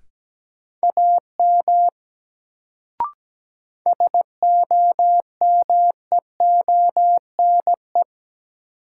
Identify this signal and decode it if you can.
Morse code, heard as a keyed tone, about 17 wpm, 700 Hz. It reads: AM SOMEONE
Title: Sentences from Top 600 Words - New Word Drill - 17wpm